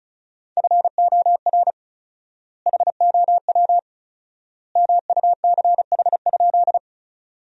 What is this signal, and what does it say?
Morse code, heard as a keyed tone, about 36 wpm, 700 Hz. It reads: FOR HOW MUCH?
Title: 4x Word-Spacing - Sentences from Top 200 Words - 35wpm